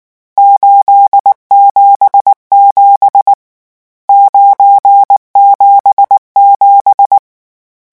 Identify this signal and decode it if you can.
Morse code, heard as a keyed tone, about 19 wpm, 800 Hz. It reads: 877 977
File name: morse mp3 / lecon12-vitesse 18